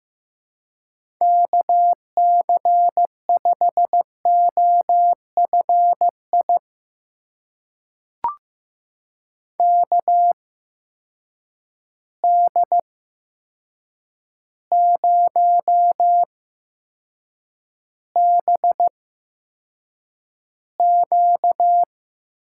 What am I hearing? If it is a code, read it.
Morse code, heard as a keyed tone, about 15 wpm, spaced as for 7 wpm, 700 Hz. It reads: KC5OFI KD0BQ
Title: USA Call Signs - 6 Characters - 8x Character Spacing - 15wpm